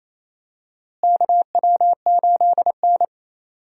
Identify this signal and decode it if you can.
Morse code, heard as a keyed tone, about 27 wpm, 700 Hz. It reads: KW8N
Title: USA Call Signs - 4 Characters - Repeated 2x - 28wpm